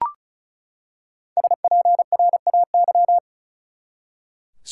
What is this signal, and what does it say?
Morse code, heard as a keyed tone, about 34 wpm, 700 Hz. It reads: SPRAY
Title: Five Letter Words - 35wpm